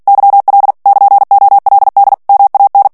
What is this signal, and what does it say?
Morse code, heard as a keyed tone, about 48 wpm, 800 Hz. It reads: YL <KN>OLD MAN
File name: morse mp3 / ITALIEN texte clair 1 vitesse 45 mots minute